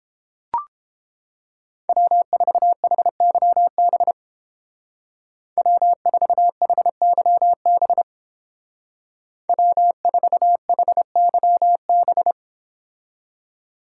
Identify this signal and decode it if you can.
Morse code, heard as a keyed tone, about 29 wpm, 700 Hz. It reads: W4HYB W4HYB W4HYB